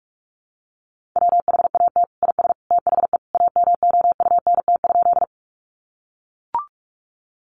Morse code, heard as a keyed tone, about 44 wpm, 700 Hz. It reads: WHAT IS THE AMOUNT?
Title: Sentences from Top 700 Words 45wpm